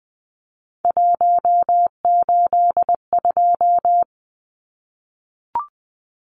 Morse code, heard as a keyed tone, about 19 wpm, 700 Hz. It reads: 182
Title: Numbers - 3-Digit 20wpm